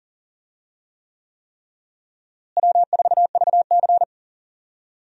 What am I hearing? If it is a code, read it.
Morse code, heard as a keyed tone, about 40 wpm, 700 Hz. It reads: W4VC
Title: USA Call Signs - 4 Characters - Repeated 3x - 40wpm